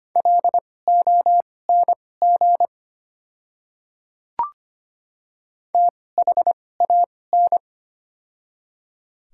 Morse code, heard as a keyed tone, about 25 wpm, spaced as for 18 wpm, 700 Hz. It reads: LONG THAN
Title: 2x Character Spacing - Top 100 Words - 25wpm